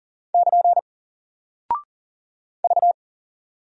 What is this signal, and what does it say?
Morse code, heard as a keyed tone, about 40 wpm, 700 Hz. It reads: <KN> V
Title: Single Character (Letter:Number:Punctuation:Prosign) Alt 40wpm